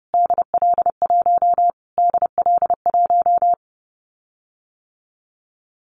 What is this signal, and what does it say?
Morse code, heard as a keyed tone, about 30 wpm, 700 Hz. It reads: DL1 DL1
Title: Callsign Prefixes for Top 25 DXCC - Repeated 3x - 30wpm